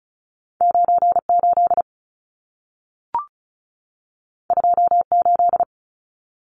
Morse code, heard as a keyed tone, about 36 wpm, 700 Hz. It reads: 98 28
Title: Numbers - 2-Digit 35wpm